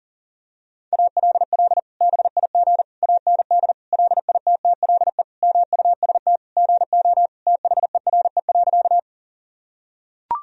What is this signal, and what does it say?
Morse code, heard as a keyed tone, about 40 wpm, 700 Hz. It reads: ALL BIG AND LITTLE MUST GO THERE.
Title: Sentences from Top 1000 Words - Review Part 1 - 40wpm